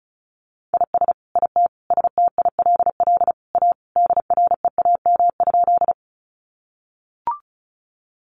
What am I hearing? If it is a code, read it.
Morse code, heard as a keyed tone, about 34 wpm, 700 Hz. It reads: IS IT STILL A DREAM?
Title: Sentences from Top 1500 Words 35wpm